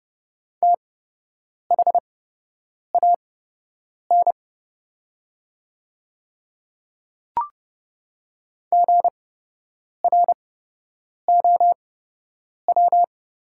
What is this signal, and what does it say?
Morse code, heard as a keyed tone, about 30 wpm, spaced as for 8 wpm, 700 Hz. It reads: THAN GROW
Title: 8x Character Spacing - Top 500 Words - 30wpm